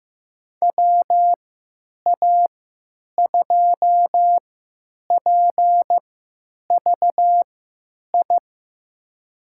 Morse code, heard as a keyed tone, about 15 wpm, spaced as for 9 wpm, 700 Hz. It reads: WA2PVI
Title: USA Call Signs - 6 Characters - 3x Character Spacing - 15wpm